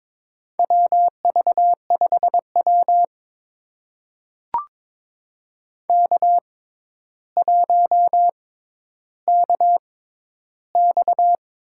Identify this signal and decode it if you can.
Morse code, heard as a keyed tone, about 22 wpm, spaced as for 11 wpm, 700 Hz. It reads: WV5W K1KX
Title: USA Call Signs - 4 Characters - 6x Character Spacing - 22wpm